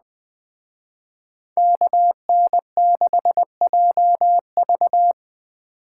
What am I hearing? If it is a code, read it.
Morse code, heard as a keyed tone, about 20 wpm, 700 Hz. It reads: KN6JV
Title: Speed Racing - USA Call Signs 17wpm